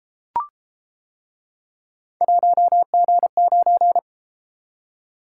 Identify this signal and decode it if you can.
Morse code, heard as a keyed tone, about 31 wpm, 700 Hz. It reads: 1G9